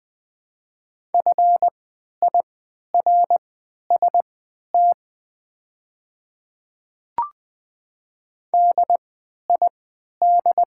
Morse code, heard as a keyed tone, about 20 wpm, spaced as for 11 wpm, 690 Hz. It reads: FIRST DID
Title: 3x Character Spacing - Top 100 Words - 20wpm